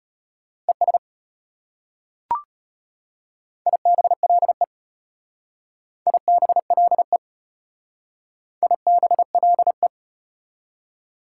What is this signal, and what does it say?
Morse code, heard as a keyed tone, about 32 wpm, 690 Hz. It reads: ES IBLE IBLE IBLE